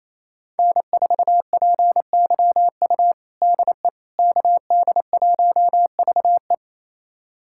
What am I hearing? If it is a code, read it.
Morse code, heard as a keyed tone, about 28 wpm, 700 Hz. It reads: N4PYU DE KD1VE
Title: USA Callsign DE Callsign - 5 Characters - 28wpm